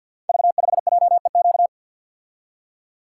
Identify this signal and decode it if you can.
Morse code, heard as a keyed tone, about 46 wpm, 700 Hz. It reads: V5JEQ
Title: Code Groups - 5 Letter:Numbers - 50wpm